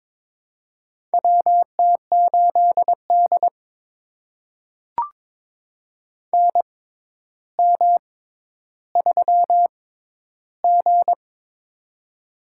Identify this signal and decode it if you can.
Morse code, heard as a keyed tone, about 22 wpm, spaced as for 11 wpm, 700 Hz. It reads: WT8D NM3G
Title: USA Call Signs - 4 Characters - 6x Character Spacing - 22wpm